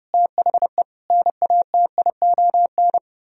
Morse code, heard as a keyed tone, about 33 wpm, 700 Hz. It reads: THE NATION